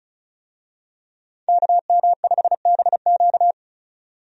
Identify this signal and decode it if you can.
Morse code, heard as a keyed tone, about 36 wpm, 700 Hz. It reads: KM5BQ
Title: USA Call Signs - 5 Characters - 35wpm